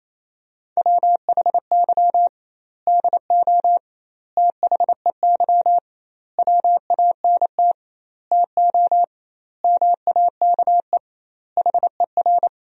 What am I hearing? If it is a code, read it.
Morse code, heard as a keyed tone, about 27 wpm, 700 Hz. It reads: WHY DO THEY WANT TO MAKE HER
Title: 2x Word-Spacing - Sentences from Top 400 Words - 28wpm